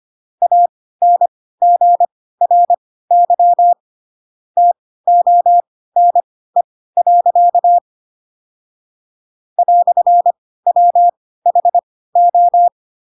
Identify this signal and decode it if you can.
Morse code, heard as a keyed tone, about 25 wpm, spaced as for 16 wpm, 700 Hz. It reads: ANGRY TONE. "WHO
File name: Pinocchio / Pinocchio ch7 10 25 ik1hge